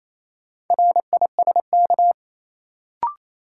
Morse code, heard as a keyed tone, about 28 wpm, 700 Hz. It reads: RISK
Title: Sentences from Top 600 Words - New Word Drill - 28wpm